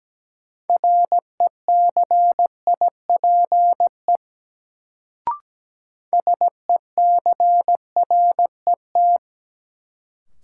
Morse code, heard as a keyed tone, about 17 wpm, 700 Hz. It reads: RECIPE SECRET
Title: Sentences from Top 1200 Words - New Word Drill Alt - 17wpm